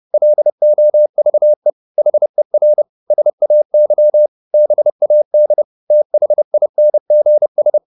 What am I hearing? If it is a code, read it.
Morse code, heard as a keyed tone, about 30 wpm, 600 Hz. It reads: LOVE HER SAY BAD THINGS
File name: the old man and the sea / the old man and the sea 1 30wpm